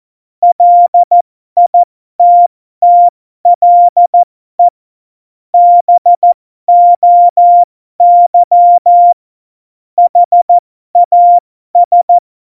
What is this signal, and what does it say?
Morse code, heard as a keyed tone, about 14 wpm, spaced as for 12 wpm, 700 Hz. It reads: LITTLE BOY HAS
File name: Morsekurs / Witze